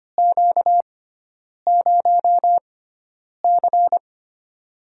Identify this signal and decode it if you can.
Morse code, heard as a keyed tone, about 25 wpm, spaced as for 9 wpm, 700 Hz. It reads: Q0C
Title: USA Call Signs - 4 Characters - 6x Character Spacing - 25wpm